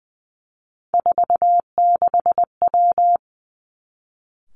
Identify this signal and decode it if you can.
Morse code, heard as a keyed tone, about 20 wpm, 700 Hz. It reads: 46W
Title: Three Challenging Letter-Number Combinations 20wpm